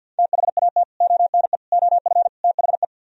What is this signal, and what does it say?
Morse code, heard as a keyed tone, about 64 wpm, 700 Hz. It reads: THAT ONE OF THE